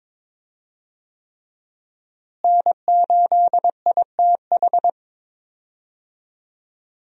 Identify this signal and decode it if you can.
Morse code, heard as a keyed tone, about 22 wpm, 700 Hz. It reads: N8ITH
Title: USA Call Signs - 5 Characters - Repeated 3x - 22wpm